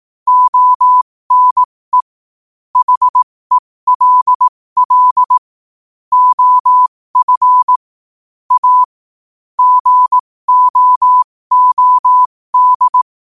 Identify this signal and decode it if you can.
Morse code, heard as a keyed tone, about 18 wpm, spaced as for 15 wpm, 1000 Hz. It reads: ONE HELL OF A GOOD
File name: qotdfiles / 15wpm